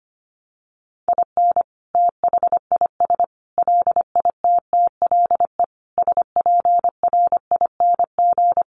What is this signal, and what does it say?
Morse code, heard as a keyed tone, about 25 wpm, 700 Hz. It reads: IN THIS LITTLE SPRING
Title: Sentences from Top 1200 Words 25wpm